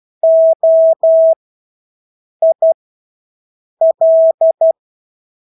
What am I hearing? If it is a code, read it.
Morse code, heard as a keyed tone, about 12 wpm, spaced as for 6 wpm, 650 Hz. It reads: OIL
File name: Engelske ord 12-6 wpm